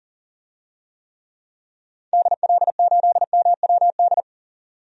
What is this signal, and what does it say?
Morse code, heard as a keyed tone, about 40 wpm, 700 Hz. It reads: DL8MWD
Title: DXCC-100 Call Signs - 6 Characters - 40wpm